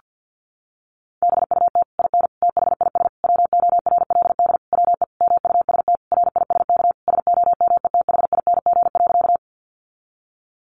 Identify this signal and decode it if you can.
Morse code, heard as a keyed tone, about 50 wpm, 700 Hz. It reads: BUT IN THIS WORLD WE MUST RISK SOMETHING.
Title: Sentences from Top 600 Words 50wpm